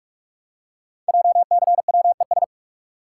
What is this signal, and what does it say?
Morse code, heard as a keyed tone, about 44 wpm, 700 Hz. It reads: JCWES
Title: Code Groups - 5 Letters - 45wpm